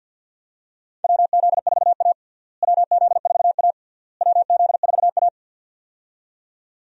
Morse code, heard as a keyed tone, about 52 wpm, 700 Hz. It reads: WZ4A WZ4A WZ4A
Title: USA Call Signs - 4 Characters - Repeated 3x - 50wpm